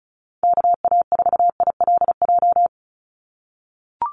Morse code, heard as a keyed tone, about 34 wpm, 700 Hz. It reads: KA4ILJ
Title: USA Call Signs 35wpm